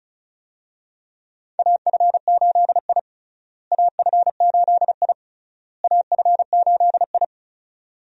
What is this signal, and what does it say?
Morse code, heard as a keyed tone, about 36 wpm, 700 Hz. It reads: AF8I AF8I AF8I
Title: USA Call Signs - 4 Characters - Repeated 3x - 35wpm